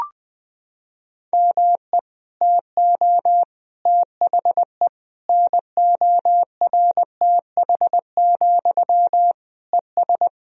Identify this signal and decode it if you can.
Morse code, heard as a keyed tone, about 20 wpm, 700 Hz. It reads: ME TO THE NORTH, ES
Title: Sentences from Top 400 Words and Abbreviations 20wpm